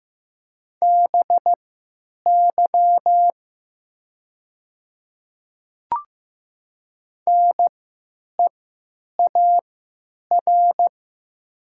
Morse code, heard as a keyed tone, about 15 wpm, spaced as for 9 wpm, 700 Hz. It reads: BY NEAR
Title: 3x Character Spacing - Top 300 Words - 15wpm